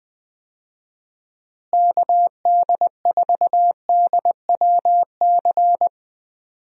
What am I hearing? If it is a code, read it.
Morse code, heard as a keyed tone, about 20 wpm, 700 Hz. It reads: KD4DWC
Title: USA Call Signs - 6 Characters - Repeated 3x - 20wpm